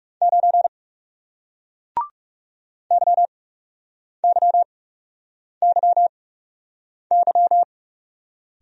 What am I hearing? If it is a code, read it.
Morse code, heard as a keyed tone, about 40 wpm, spaced as for 8 wpm, 700 Hz. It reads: 9 YYYY